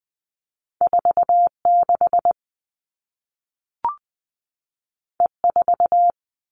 Morse code, heard as a keyed tone, about 20 wpm, 700 Hz. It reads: 46 E4
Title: All Cut Numbers - 2-Digit 20wpm